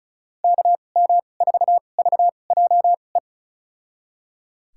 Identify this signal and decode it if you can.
Morse code, heard as a keyed tone, about 34 wpm, spaced as for 25 wpm, 700 Hz. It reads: KM4VJE
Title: USA Call Signs - 6 Characters - 2x Character Spacing - 35wpm